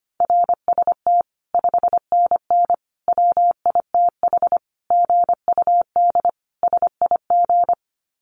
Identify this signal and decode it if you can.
Morse code, heard as a keyed tone, about 25 wpm, 700 Hz. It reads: RST 5NN WITH GUD SIG
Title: Common QSO Phrases 25wpm